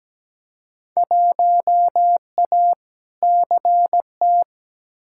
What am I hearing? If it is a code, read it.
Morse code, heard as a keyed tone, about 17 wpm, 700 Hz. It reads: 1A CT
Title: Field Day Exchanges Encore - 17wpm